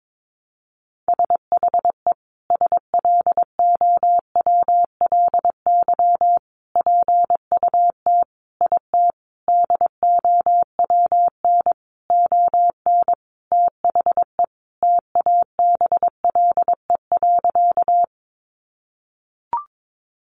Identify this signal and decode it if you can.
Morse code, heard as a keyed tone, about 22 wpm, 700 Hz. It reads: SHE SLOWLY PUT IT DOWN ON THE TABLE.
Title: Sentences from Top 1100 Words 22wpm